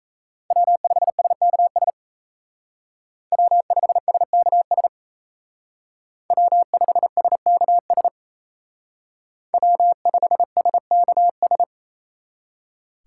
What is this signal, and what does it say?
Morse code, heard as a keyed tone, about 33 wpm, 700 Hz. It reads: W5SKS W5SKS W5SKS W5SKS
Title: Speed Racing - USA Call Signs 28wpm